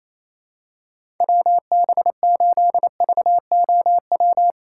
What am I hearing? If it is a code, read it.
Morse code, heard as a keyed tone, about 28 wpm, 700 Hz. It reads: WB8VOW
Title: USA Call Signs - 6 Characters - 2x Character Spacing - 28wpm